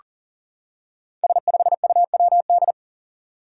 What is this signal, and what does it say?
Morse code, heard as a keyed tone, about 40 wpm, 700 Hz. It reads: S5UWD
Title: Five Challenging Letter-Number Combinations 40wpm